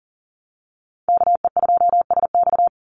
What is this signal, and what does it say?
Morse code, heard as a keyed tone, about 40 wpm, 700 Hz. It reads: KE2SX
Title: Call Signs Repeated 2x - 40wpm